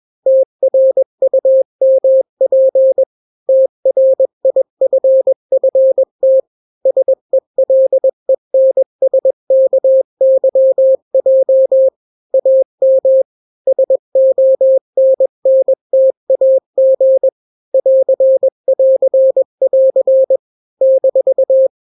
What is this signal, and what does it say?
Morse code, heard as a keyed tone, about 21 wpm, 530 Hz. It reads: TRUMP TRIFFT SELENSKYJ AM SONNTAG <AR><AR><AR> -
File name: daily / 20wpm